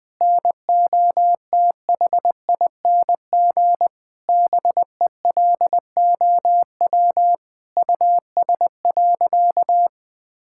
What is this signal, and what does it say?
Morse code, heard as a keyed tone, about 20 wpm, 700 Hz. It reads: NOTHING BELOW US.
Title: Sentences from Top 1000 Words - Review Part 5 - 20wpm